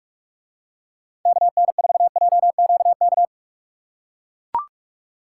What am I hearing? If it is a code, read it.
Morse code, heard as a keyed tone, about 41 wpm, 700 Hz. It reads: KN4JQK